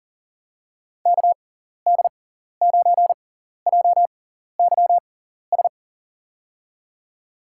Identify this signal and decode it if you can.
Morse code, heard as a keyed tone, about 40 wpm, spaced as for 14 wpm, 700 Hz. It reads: KD9JYS
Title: USA Call Signs - 6 Characters - 6x Character Spacing - 40wpm